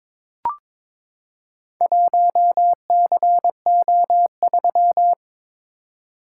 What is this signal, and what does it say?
Morse code, heard as a keyed tone, about 22 wpm, 700 Hz. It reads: 1CO3
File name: practice / 22wpm / Four Letter-Number Combinations - 22wpm